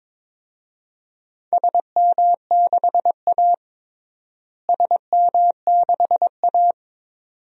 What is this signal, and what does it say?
Morse code, heard as a keyed tone, about 22 wpm, 700 Hz. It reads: SM6A SM6A
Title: DXCC-100 Call Signs - 4 Characters - Repeated 2x - 22wpm